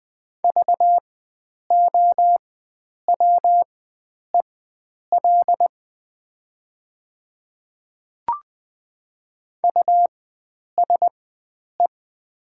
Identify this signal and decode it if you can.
Morse code, heard as a keyed tone, about 20 wpm, spaced as for 9 wpm, 700 Hz. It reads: VOWEL USE
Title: 4x Character Spacing - Top 500 Words - 20wpm